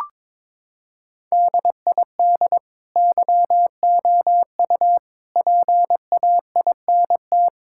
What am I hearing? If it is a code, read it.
Morse code, heard as a keyed tone, about 21 wpm, 700 Hz. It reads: DID YOU PAINT
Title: Sentences from Top 1000 Words - Review Part 6 - 22wpm